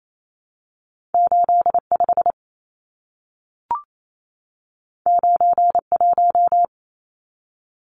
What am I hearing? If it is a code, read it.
Morse code, heard as a keyed tone, about 28 wpm, 700 Hz. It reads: 85 91
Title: Numbers - 2-Digit 28wpm